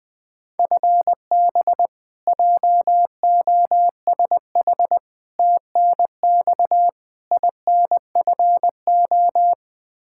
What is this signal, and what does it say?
Morse code, heard as a keyed tone, about 21 wpm, 700 Hz. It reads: FB JOSH TNX INFO
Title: Practice QSO - Exchange 3 - More Info - 20wpm